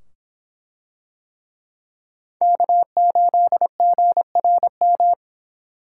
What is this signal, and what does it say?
Morse code, heard as a keyed tone, about 25 wpm, 700 Hz. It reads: K8GRM